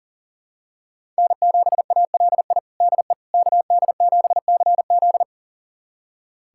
Morse code, heard as a keyed tone, about 40 wpm, 700 Hz. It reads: N7ALI DE KD7CZ
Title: USA Callsign DE Callsign - 5 Characters - 40wpm